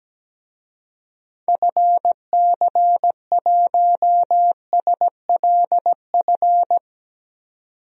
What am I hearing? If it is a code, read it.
Morse code, heard as a keyed tone, about 17 wpm, 700 Hz. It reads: FC1SLF
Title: DXCC-100 Call Signs - 6 Characters - 17wpm